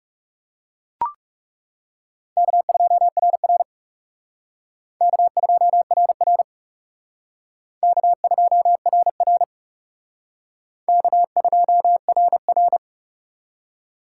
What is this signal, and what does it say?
Morse code, heard as a keyed tone, about 37 wpm, 700 Hz. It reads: K2RR K2RR K2RR K2RR
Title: Speed Racing - USA 1x2 Call Signs 30wpm